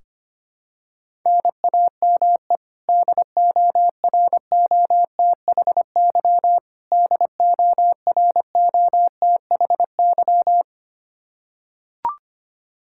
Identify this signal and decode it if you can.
Morse code, heard as a keyed tone, about 25 wpm, 700 Hz. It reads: NAME DOROTHY DOROTHY
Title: Practice QSO - Exchange 1 - Beginning - 25wpm